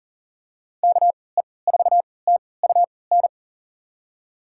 Morse code, heard as a keyed tone, about 40 wpm, spaced as for 23 wpm, 700 Hz. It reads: KE4TUN
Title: USA Call Signs - 6 Characters - 3x Character Spacing - 40wpm